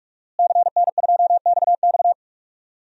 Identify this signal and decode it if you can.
Morse code, heard as a keyed tone, about 43 wpm, 700 Hz. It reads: CN2XX
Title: DXCC-100 Call Signs - 5 Characters - Repeated 3x - 45wpm